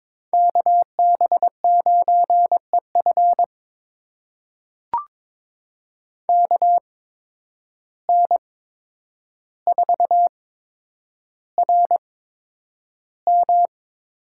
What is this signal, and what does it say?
Morse code, heard as a keyed tone, about 22 wpm, spaced as for 9 wpm, 700 Hz. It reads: KB9EF KN4RM
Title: USA Call Signs - 5 Characters - 8x Character Spacing - 22wpm